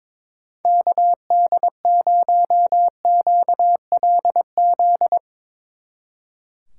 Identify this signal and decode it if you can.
Morse code, heard as a keyed tone, about 24 wpm, 700 Hz. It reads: KD0QLZ
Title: Speed Racing - USA 2x3 Call Signs 22wpm